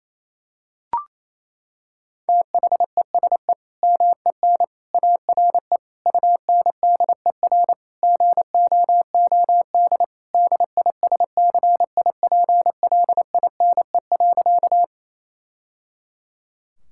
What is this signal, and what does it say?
Morse code, heard as a keyed tone, about 28 wpm, 700 Hz. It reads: THESE MEN ARE UNDER GOOD DISCIPLINE.
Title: Sentences from Top 1000 Words - Review Part 6 - 28wpm